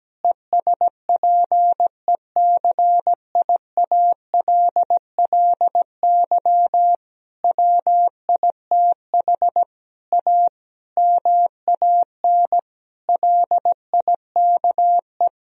Morse code, heard as a keyed tone, about 17 wpm, 700 Hz. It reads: ESPECIALLY WITH A MAN LIKE
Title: Sentences from Top 1000 Words - Review Part 4 - 17wpm